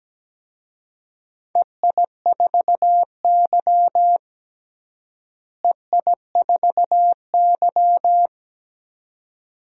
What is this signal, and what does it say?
Morse code, heard as a keyed tone, about 17 wpm, 700 Hz. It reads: EI4Y EI4Y
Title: DXCC-100 Call Signs - 4 Characters - Repeated 2x - 17wpm